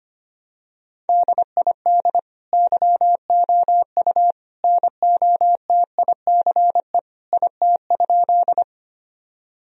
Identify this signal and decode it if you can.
Morse code, heard as a keyed tone, about 25 wpm, 700 Hz. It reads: DID YOU NOTICE IT?
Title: Sentences from Top 1000 Words - Review Part 2 - 25wpm